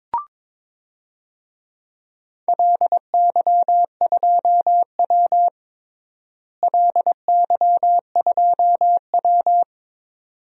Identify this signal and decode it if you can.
Morse code, heard as a keyed tone, about 22 wpm, 700 Hz. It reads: LY2W LY2W
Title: DXCC-100 Call Signs - 4 Characters - Repeated 2x - 22wpm